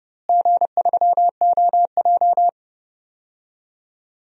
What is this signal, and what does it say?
Morse code, heard as a keyed tone, about 31 wpm, 700 Hz. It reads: G3OJ
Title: DXCC-100 Call Signs - 4 Characters - Repeated 2x - 30wpm